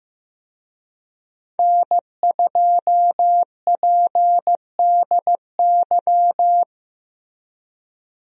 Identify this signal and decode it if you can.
Morse code, heard as a keyed tone, about 15 wpm, 700 Hz. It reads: N2PDY
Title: USA Call Signs - 5 Characters - 15wpm